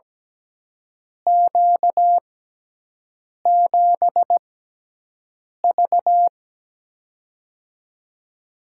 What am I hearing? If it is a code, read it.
Morse code, heard as a keyed tone, about 17 wpm, spaced as for 6 wpm, 700 Hz. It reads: Q7V
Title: USA Call Signs - 4 Characters - 6x Character Spacing - 17wpm